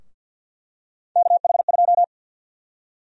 Morse code, heard as a keyed tone, about 51 wpm, 700 Hz. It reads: KH2
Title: USA Call Sign Prefixes - 50wpm